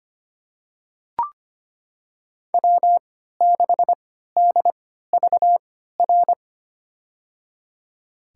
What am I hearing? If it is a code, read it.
Morse code, heard as a keyed tone, about 25 wpm, spaced as for 14 wpm, 700 Hz. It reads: W6DVR
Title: USA Call Signs - 5 Characters - 3x Character Spacing - 25wpm